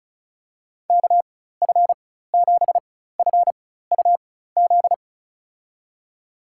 Morse code, heard as a keyed tone, about 36 wpm, spaced as for 16 wpm, 700 Hz. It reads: KF7FUZ
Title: USA Call Signs - 6 Characters - 4x Character Spacing - 35wpm